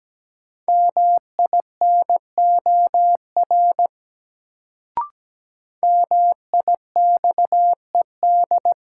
Morse code, heard as a keyed tone, about 17 wpm, 700 Hz. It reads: MINOR MIXED
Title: Sentences from Top 1500 Words - New Word Drill Alt - 17wpm